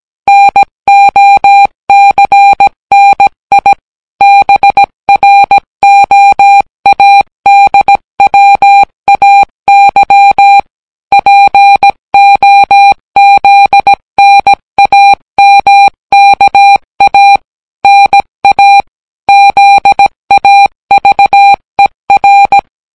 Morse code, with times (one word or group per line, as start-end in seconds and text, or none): 0.3-3.7 NOCNI
4.2-10.6 BROADWAY
11.1-17.4 POZNAMKA
17.8-18.8 NA
19.3-22.6 ZAVER